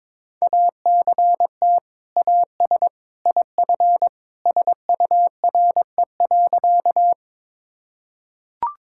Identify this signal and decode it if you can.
Morse code, heard as a keyed tone, about 22 wpm, 700 Hz. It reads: ACT AS IF SURE.
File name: practice / 22wpm / Sentences with 2-Word Phrases - 22wpm